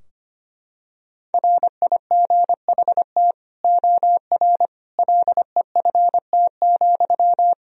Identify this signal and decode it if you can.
Morse code, heard as a keyed tone, about 25 wpm, 700 Hz. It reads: RIGHT OR LEFT,
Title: Sentences from Top 1000 Words - Review Part 4 - 25wpm